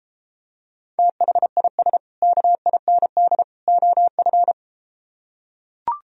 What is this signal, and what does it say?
Morse code, heard as a keyed tone, about 33 wpm, 700 Hz. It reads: THIS KIND OF